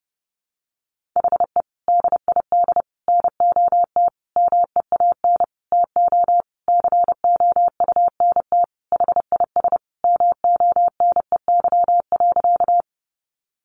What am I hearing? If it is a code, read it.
Morse code, heard as a keyed tone, about 30 wpm, 700 Hz. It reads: HE DID NOT MEAN TO COUNT HIS MONEY.
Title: Sentences from Top 1500 Words 30wpm